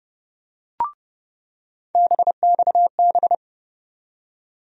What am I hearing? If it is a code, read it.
Morse code, heard as a keyed tone, about 30 wpm, 700 Hz. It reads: BXB
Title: USA Call Sign Postfixes - 3 Letters 30wpm